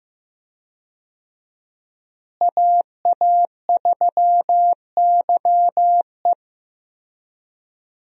Morse code, heard as a keyed tone, about 15 wpm, 700 Hz. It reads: AA3YE
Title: USA Call Signs - 5 Characters - Repeated 3x - 15wpm